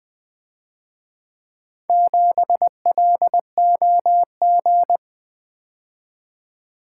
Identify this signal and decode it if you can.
Morse code, heard as a keyed tone, about 24 wpm, 700 Hz. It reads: 7LOG
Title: Speed-Racing - Four Letter-Number Combinations - 20wpm